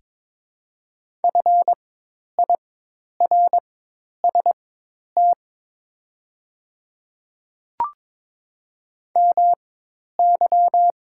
Morse code, heard as a keyed tone, about 22 wpm, spaced as for 10 wpm, 700 Hz. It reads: FIRST MY
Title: 4x Character Spacing - Top 300 Words - 22wpm